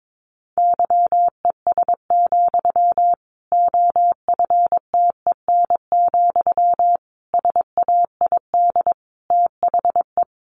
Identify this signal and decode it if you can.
Morse code, heard as a keyed tone, about 22 wpm, 700 Hz. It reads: YES, OFTEN, SAID THE
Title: Sentences from Top 1200 Words 22wpm